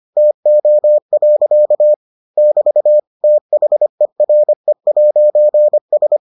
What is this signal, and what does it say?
Morse code, heard as a keyed tone, about 25 wpm, 600 Hz. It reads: TO. <BT> THERE'S